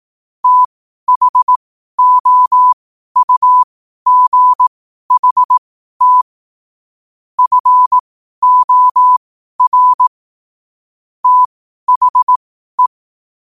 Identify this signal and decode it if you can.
Morse code, heard as a keyed tone, about 18 wpm, spaced as for 12 wpm, 1000 Hz. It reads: THOUGHT FOR THE
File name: qotdfiles / 12wpm